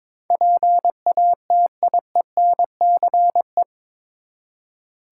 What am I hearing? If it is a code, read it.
Morse code, heard as a keyed tone, about 22 wpm, 700 Hz. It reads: PATIENCE
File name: practice / 22wpm / Eight Letter Words - 22wpm